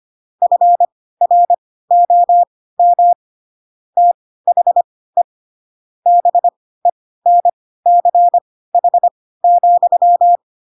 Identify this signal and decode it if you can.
Morse code, heard as a keyed tone, about 25 wpm, spaced as for 16 wpm, 700 Hz. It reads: FROM THE BENCH,
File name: Pinocchio / Pinocchio ch4 10 25 ik1hge